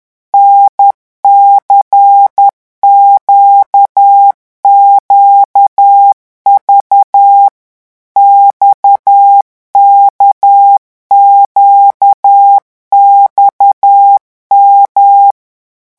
Morse code, heard as a keyed tone, about 10 wpm, 800 Hz. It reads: NCQQV XKQXM
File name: morse mp3 / revision3-vitesse 10 mots